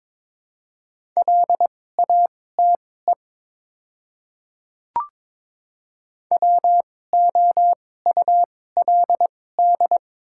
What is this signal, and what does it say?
Morse code, heard as a keyed tone, about 22 wpm, spaced as for 16 wpm, 700 Hz. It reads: LATE WOULD
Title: 2x Character Spacing - Top 300 Words - 22wpm